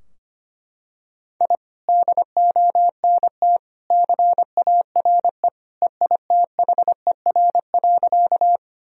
Morse code, heard as a keyed tone, about 25 wpm, 700 Hz. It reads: I DONT CARE EITHER.
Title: Sentences from Top 1000 Words - Review Part 4 - 25wpm